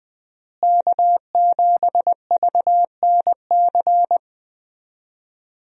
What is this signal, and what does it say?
Morse code, heard as a keyed tone, about 20 wpm, 700 Hz. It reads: K7VNC
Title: USA Call Signs - 5 Characters - Repeated 3x - 20wpm